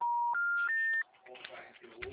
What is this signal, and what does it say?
Special information tone, with speech.
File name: out- -2016- - - .455